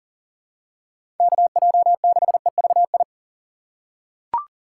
Morse code, heard as a keyed tone, about 40 wpm, 700 Hz. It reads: KJ6EVI
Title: USA Call Signs - 6 Characters - 4x Character Spacing - 40wpm